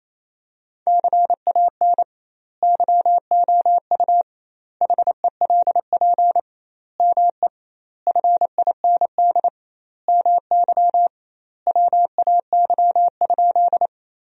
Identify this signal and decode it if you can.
Morse code, heard as a keyed tone, about 28 wpm, 700 Hz. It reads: CAN YOU HELP ME FIND MY WAY?
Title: 2x Word-Spacing - Sentences from Top 200 Words - 28wpm